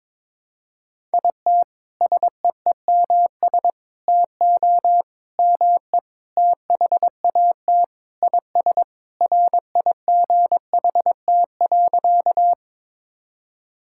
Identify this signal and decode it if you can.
Morse code, heard as a keyed tone, about 22 wpm, 700 Hz. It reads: IT SEEMS TO ME THAT IS RIGHT.
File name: practice / 22wpm / Sentences with 4-Word Phrases - 22wpm